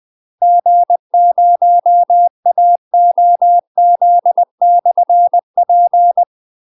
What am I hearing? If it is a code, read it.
Morse code, heard as a keyed tone, about 20 wpm, 700 Hz. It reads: G0AOZ/P
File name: UK MASTER QSOs TRAINING / 20WPM / UK MASTER QSOs 0003